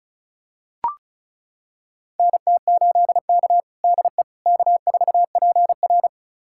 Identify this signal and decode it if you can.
Morse code, heard as a keyed tone, about 36 wpm, 700 Hz. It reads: NT8K DE K4PR
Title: USA Callsign DE Callsign - 4 Characters - 35wpm